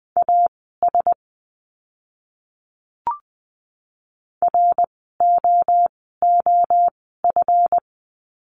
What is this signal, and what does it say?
Morse code, heard as a keyed tone, about 20 wpm, spaced as for 15 wpm, 700 Hz. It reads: AS ROOF
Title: 139 Lesson 23f - Letter f - Words 20wpm